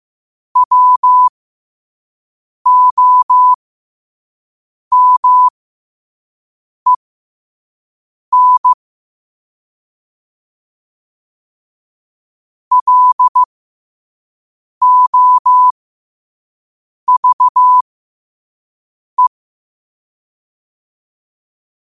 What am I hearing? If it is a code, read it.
Morse code, heard as a keyed tone, about 15 wpm, spaced as for 5 wpm, 1000 Hz. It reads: WOMEN LOVE
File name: qotdfiles / 5wpm